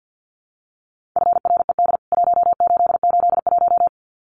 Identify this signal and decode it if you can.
Morse code, heard as a keyed tone, about 50 wpm, 700 Hz. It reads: FRED 1881
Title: CWT Exchange 50wpm